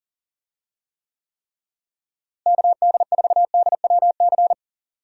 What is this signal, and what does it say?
Morse code, heard as a keyed tone, about 37 wpm, 700 Hz. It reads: KD4DWC
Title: USA Call Signs - 6 Characters - Repeated 3x - 40wpm